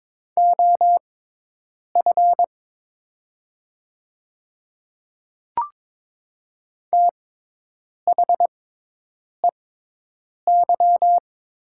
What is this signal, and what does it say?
Morse code, heard as a keyed tone, about 22 wpm, spaced as for 8 wpm, 700 Hz. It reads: OF THEY
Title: 6x Character Spacing - Top 100 Words - 22wpm